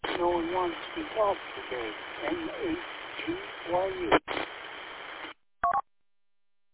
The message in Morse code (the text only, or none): none